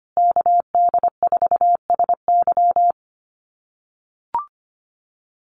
Morse code, heard as a keyed tone, about 25 wpm, 700 Hz. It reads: KD4SY
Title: USA Call Signs Repeated 3x 25wpm